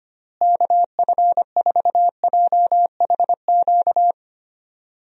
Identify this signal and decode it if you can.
Morse code, heard as a keyed tone, about 25 wpm, 700 Hz. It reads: KF4JHQ
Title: USA Call Signs - 6 Characters - Repeated 3x - 25wpm